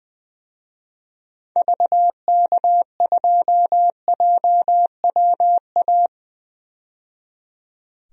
Morse code, heard as a keyed tone, about 20 wpm, 700 Hz. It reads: VK2JWA
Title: DXCC-100 Call Signs - 6 Characters - 20wpm